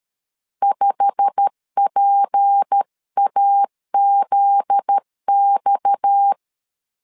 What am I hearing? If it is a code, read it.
Morse code, heard as a keyed tone, about 13 wpm, 800 Hz. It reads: 5PAZX
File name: CW / DET / 1 9 1500C S B 12wpm 3819